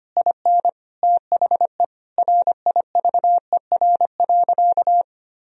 Morse code, heard as a keyed tone, about 25 wpm, 700 Hz. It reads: IN THE RIVER.
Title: Sentences from Top 1000 Words - Review Part 4 - 25wpm